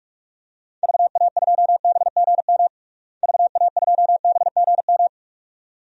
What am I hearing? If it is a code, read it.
Morse code, heard as a keyed tone, about 46 wpm, 700 Hz. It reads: VA2BGM VA2BGM
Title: DXCC-100 Call Signs - 6 Characters - Repeated 2x - 45wpm